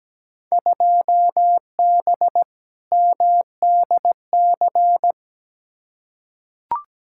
Morse code, heard as a keyed tone, about 17 wpm, 700 Hz. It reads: 2B MDC
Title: Field Day Exchanges - 17wpm